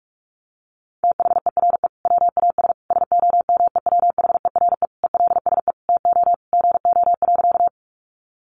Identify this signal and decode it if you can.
Morse code, heard as a keyed tone, about 44 wpm, 700 Hz. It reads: THERE WAS SOMEWHERE ELSE TO GO.
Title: Sentences from Top 1400 Words 45wpm